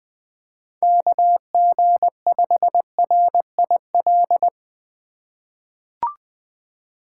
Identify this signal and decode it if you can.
Morse code, heard as a keyed tone, about 21 wpm, 700 Hz. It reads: KG5RIL
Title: USA Call Signs - 6 Characters - 2x Character Spacing - 20wpm